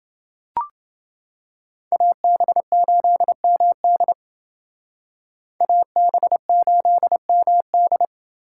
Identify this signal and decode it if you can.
Morse code, heard as a keyed tone, about 28 wpm, 700 Hz. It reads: AB8MD AB8MD